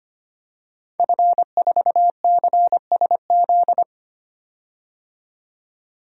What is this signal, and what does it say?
Morse code, heard as a keyed tone, about 25 wpm, 700 Hz. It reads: F4CSZ
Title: DXCC-100 Call Signs - 5 Characters - 25wpm